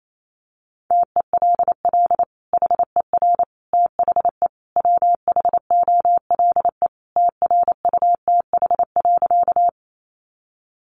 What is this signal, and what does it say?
Morse code, heard as a keyed tone, about 28 wpm, 700 Hz. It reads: TELL HER THE WHOLE TRUTH.
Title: Sentences from Top 800 Words 28wpm